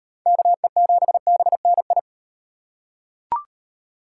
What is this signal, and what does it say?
Morse code, heard as a keyed tone, about 38 wpm, 700 Hz. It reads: KE7BNI